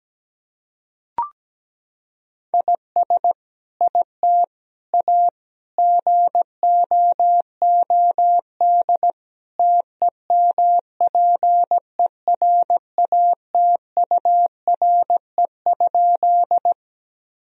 Morse code, heard as a keyed tone, about 17 wpm, 700 Hz. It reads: IS IT A GOOD TEMPERATURE?
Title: Sentences from Top 1000 Words - Review Part 4 - 17wpm